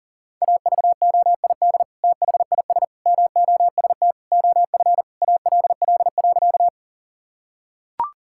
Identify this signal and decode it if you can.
Morse code, heard as a keyed tone, about 40 wpm, 700 Hz. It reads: AVOID THIS MOST OF ALL.
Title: Sentences from Top 1000 Words - Review Part 6 - 40wpm